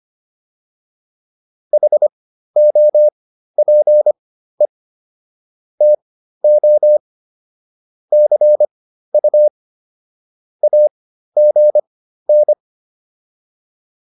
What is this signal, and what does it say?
Morse code, heard as a keyed tone, about 25 wpm, spaced as for 13 wpm, 600 Hz. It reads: HOPE TO CU AGN